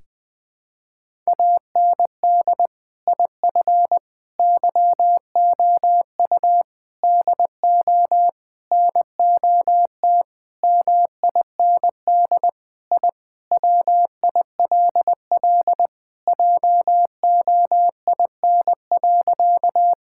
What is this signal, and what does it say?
Morse code, heard as a keyed tone, about 20 wpm, 700 Hz. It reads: AND IF YOU DO NOT MIND I WILL JOIN.
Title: Sentences with 5-Word Phrases - 20wpm